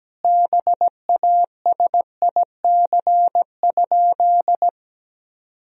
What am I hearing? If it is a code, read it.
Morse code, heard as a keyed tone, about 17 wpm, 700 Hz. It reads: BASIC?
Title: Sentences from Top 1000 Words - Review Part 3 - 17wpm